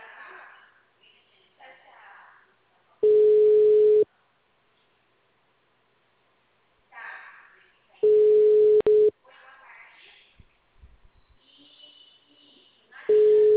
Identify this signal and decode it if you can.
Ringback tone (Europe, 425 Hz), with speech.